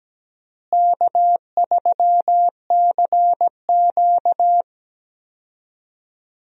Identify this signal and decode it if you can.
Morse code, heard as a keyed tone, about 17 wpm, 700 Hz. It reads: K3CQ
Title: USA Call Signs - 4 Characters - Repeated 3x - 17wpm